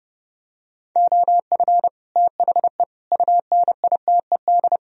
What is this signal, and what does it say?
Morse code, heard as a keyed tone, about 29 wpm, 700 Hz. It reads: OF THE UNITED